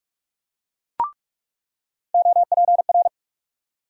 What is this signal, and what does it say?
Morse code, heard as a keyed tone, about 46 wpm, 700 Hz. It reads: OPR
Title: USA Call Sign Postfixes - 3 Letters 45wpm